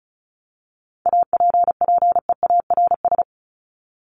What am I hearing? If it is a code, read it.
Morse code, heard as a keyed tone, about 34 wpm, 700 Hz. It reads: APPEARS
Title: ES:S Suffix - Top 100 Words with Base Words 35wpm